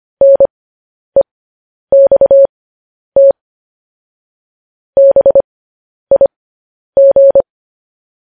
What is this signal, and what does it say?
Morse code, heard as a keyed tone, about 25 wpm, spaced as for 10 wpm, 550 Hz. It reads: NEXT BIG
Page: News Headlines in Morse Code at 10 WPM